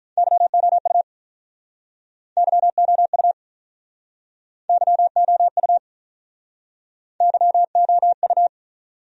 Morse code, heard as a keyed tone, about 47 wpm, 700 Hz. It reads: YOU YOU YOU YOU